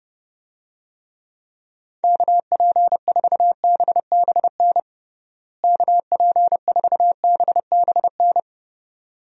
Morse code, heard as a keyed tone, about 30 wpm, 700 Hz. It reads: KP4BBN KP4BBN
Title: DXCC-100 Call Signs - 6 Characters - Repeated 2x - 30wpm